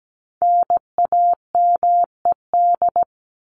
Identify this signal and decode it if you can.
Morse code, heard as a keyed tone, about 17 wpm, 700 Hz. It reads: NAMED
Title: ED Suffix - Top 100 Words with Base Words 17wpm